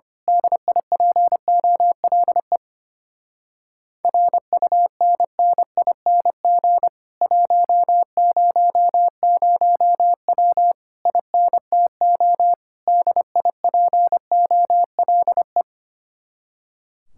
Morse code, heard as a keyed tone, about 25 wpm, 700 Hz. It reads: DIPOLE RUNNING 100W INTO DIPOLE
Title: Speed-Racing Common QSO Phrases - 25wpm